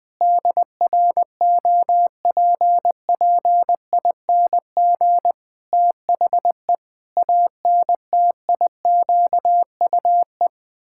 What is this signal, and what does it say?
Morse code, heard as a keyed tone, about 20 wpm, 700 Hz. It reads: DROPPING THE ANTIQUE
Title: Pangram Challenge v2 - 20wpm